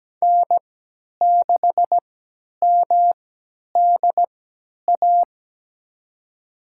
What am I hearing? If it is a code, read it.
Morse code, heard as a keyed tone, about 17 wpm, spaced as for 10 wpm, 700 Hz. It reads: N6MDA
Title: USA Call Signs - 5 Characters - 3x Character Spacing - 17wpm